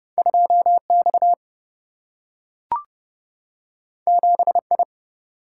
Morse code, heard as a keyed tone, about 30 wpm, 700 Hz. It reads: 2 X 7 I
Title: Single-Letter Single-Number 30wpm